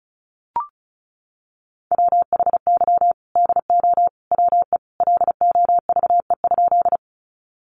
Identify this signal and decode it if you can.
Morse code, heard as a keyed tone, about 34 wpm, 700 Hz. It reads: WHY DO WE LOVE?
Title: Sentences from Top 400 Words 35wpm